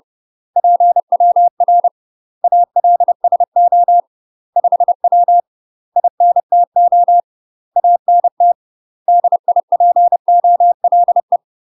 Morse code, heard as a keyed tone, about 30 wpm, 700 Hz. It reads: PWR ALSO 5W INTO ANT DIPOLE
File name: UK MASTER QSOs TRAINING / 30WPM / UK MASTER QSOs 0027